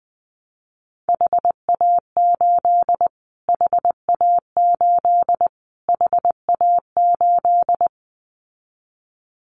Morse code, heard as a keyed tone, about 20 wpm, 700 Hz. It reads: HA8 HA8 HA8
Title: Callsign Prefixes for Top 25 DXCC - Repeated 3x - 20wpm